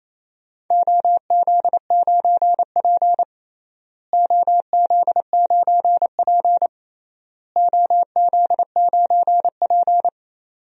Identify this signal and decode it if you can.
Morse code, heard as a keyed tone, about 28 wpm, 700 Hz. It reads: OZ9P OZ9P OZ9P
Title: DXCC-100 Call Signs - 4 Characters - Repeated 3x - 28wpm